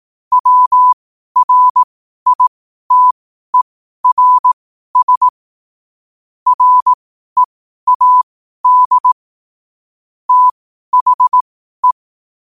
Morse code, heard as a keyed tone, about 18 wpm, spaced as for 12 wpm, 1000 Hz. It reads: WRITERS READ THE
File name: qotdfiles / 12wpm